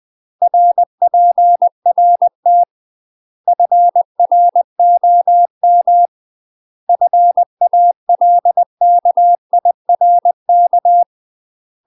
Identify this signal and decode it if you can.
Morse code, heard as a keyed tone, about 20 wpm, 700 Hz. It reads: RPRT FROM FALKIRK
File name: UK MASTER QSOs TRAINING / 20WPM / UK MASTER QSOs 0047